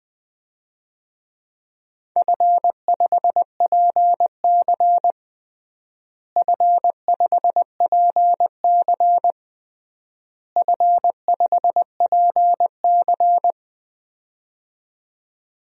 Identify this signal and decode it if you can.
Morse code, heard as a keyed tone, about 21 wpm, 700 Hz. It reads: F5PC F5PC F5PC
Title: DXCC-100 Call Signs - 4 Characters - Repeated 3x - 20wpm